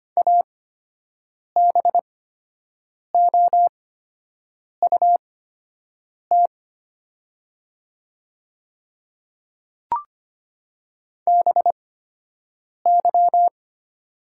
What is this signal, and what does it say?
Morse code, heard as a keyed tone, about 25 wpm, spaced as for 7 wpm, 700 Hz. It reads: ABOUT BY
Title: 8x Character Spacing - Top 100 Words - 25wpm